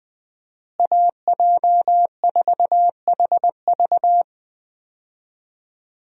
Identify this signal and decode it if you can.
Morse code, heard as a keyed tone, about 20 wpm, 700 Hz. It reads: AJ4HV
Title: USA Call Signs - 5 Characters - Repeated 3x - 20wpm